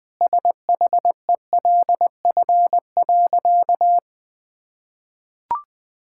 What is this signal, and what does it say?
Morse code, heard as a keyed tone, about 20 wpm, 700 Hz. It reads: SHELF.
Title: Sentences with 2-Word Phrases - 20wpm